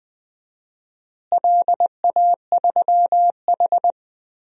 Morse code, heard as a keyed tone, about 21 wpm, 700 Hz. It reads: LA3H
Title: DXCC-100 Call Signs - 4 Characters - Repeated 3x - 20wpm